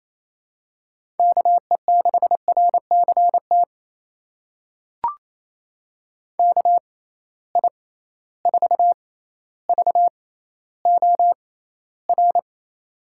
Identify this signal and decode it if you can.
Morse code, heard as a keyed tone, about 28 wpm, spaced as for 14 wpm, 700 Hz. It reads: KE6RCT KI4VOR
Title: USA Call Signs - 6 Characters - 6x Character Spacing - 28wpm